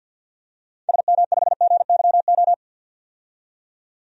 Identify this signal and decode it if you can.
Morse code, heard as a keyed tone, about 50 wpm, 700 Hz. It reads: SM5GYO
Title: DXCC-100 Call Signs - 6 Characters - 50wpm